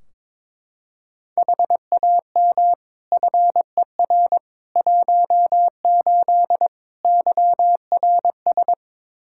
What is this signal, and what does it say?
Morse code, heard as a keyed tone, about 22 wpm, 700 Hz. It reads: HAM FER 18 YRS
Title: Practice QSO - Exchange 3 - More Info - 22wpm